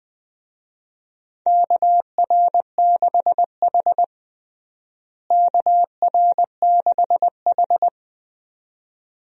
Morse code, heard as a keyed tone, about 20 wpm, 700 Hz. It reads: KR6H KR6H
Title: USA Call Signs - 4 Characters - Repeated 2x - 20wpm